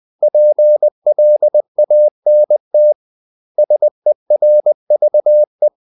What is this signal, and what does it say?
Morse code, heard as a keyed tone, about 20 wpm, 600 Hz. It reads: PLANT SERVE
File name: CW / Practice / 500MostCommonEnglishWords20